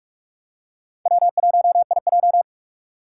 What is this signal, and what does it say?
Morse code, heard as a keyed tone, about 46 wpm, 700 Hz. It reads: W1IJ
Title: USA Call Signs - 2x Character Spacing - 45wpm